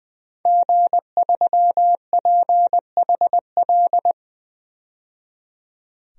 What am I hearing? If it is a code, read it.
Morse code, heard as a keyed tone, about 21 wpm, 700 Hz. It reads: G3PHL
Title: DXCC-100 Call Signs - 5 Characters - 20wpm